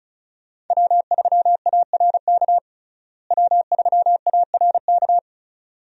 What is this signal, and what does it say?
Morse code, heard as a keyed tone, about 36 wpm, 700 Hz. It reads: W3ARK W3ARK
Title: USA Call Signs - 5 Characters - Repeated 2x - 35wpm